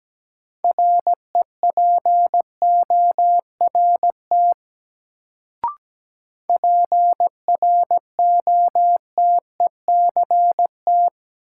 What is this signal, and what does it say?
Morse code, heard as a keyed tone, about 17 wpm, 700 Hz. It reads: REPORT PROTECT
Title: Sentences from Top 900 Words - New Word Drill - 17wpm